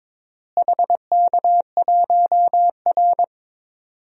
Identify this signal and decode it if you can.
Morse code, heard as a keyed tone, about 22 wpm, 700 Hz. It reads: HK1R
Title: DXCC-100 Call Signs - 4 Characters - Repeated 3x - 22wpm